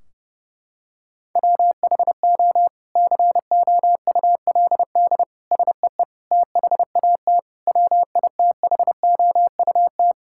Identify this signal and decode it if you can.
Morse code, heard as a keyed tone, about 29 wpm, 700 Hz. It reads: WHO COULD SEE THAT WITHOUT
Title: Sentences from Top 1600 Words 30wpm